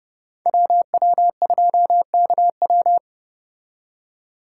Morse code, heard as a keyed tone, about 29 wpm, 700 Hz. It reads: WW2KW